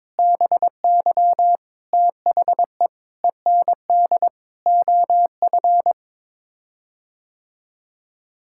Morse code, heard as a keyed tone, about 22 wpm, 700 Hz. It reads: BY THE END OF
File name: practice / 22wpm / Sets of 8 Words - 22wpm